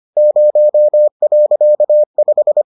Morse code, heard as a keyed tone, about 25 wpm, 600 Hz. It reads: 0.5